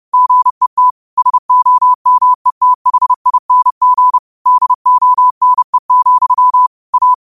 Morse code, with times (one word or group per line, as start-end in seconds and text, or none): 0.1-0.9 GET
1.2-4.2 SOMETHING
4.5-6.7 DONE,
6.9-7.1 A